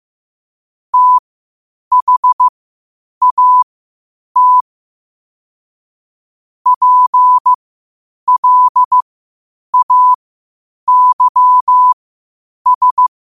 Morse code, heard as a keyed tone, about 15 wpm, spaced as for 8 wpm, 1000 Hz. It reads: THAT PLAYS
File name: qotdfiles / 8wpm